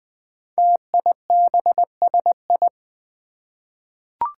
Morse code, heard as a keyed tone, about 20 wpm, 700 Hz. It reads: TIBSI
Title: Five Challenging Letter-Number Combinations 20wpm